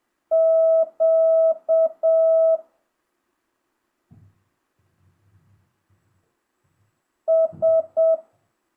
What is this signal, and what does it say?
Morse code, heard as a keyed tone, about 7 wpm, 650 Hz. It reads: Q S